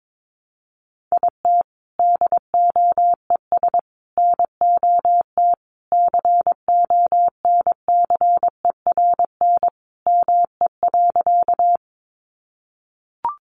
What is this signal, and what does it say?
Morse code, heard as a keyed tone, about 22 wpm, 700 Hz. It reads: IT DOES NOT CONCERN ME.
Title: Sentences from Top 1000 Words 22wpm